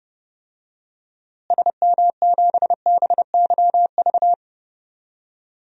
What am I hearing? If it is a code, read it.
Morse code, heard as a keyed tone, about 30 wpm, 700 Hz. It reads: SM7BYV
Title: DXCC-100 Call Signs - 6 Characters - Repeated 3x - 30wpm